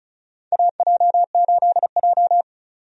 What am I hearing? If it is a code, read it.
Morse code, heard as a keyed tone, about 36 wpm, 700 Hz. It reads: AJ8J
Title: USA Call Signs - 4 Characters - 35wpm